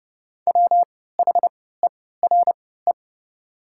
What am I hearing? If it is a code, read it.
Morse code, heard as a keyed tone, about 30 wpm, spaced as for 17 wpm, 700 Hz. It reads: WHERE
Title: 3x Character Spacing - Top 300 Words - 30wpm